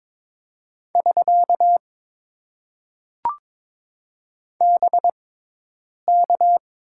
Morse code, heard as a keyed tone, about 22 wpm, spaced as for 8 wpm, 700 Hz. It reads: <SK> BK
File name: Top 90 QSO Elements - 6x Character Spacing - 22wpm